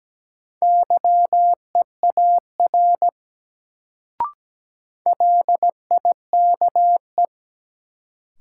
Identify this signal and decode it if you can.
Morse code, heard as a keyed tone, about 17 wpm, 700 Hz. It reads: YEAR LIKE
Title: Sets of 2 Words - New Word Drill - 17wpm